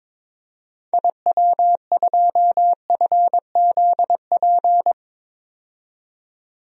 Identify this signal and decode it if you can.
Morse code, heard as a keyed tone, about 22 wpm, 700 Hz. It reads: IW2FZP
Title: DXCC-100 Call Signs - 6 Characters - 22wpm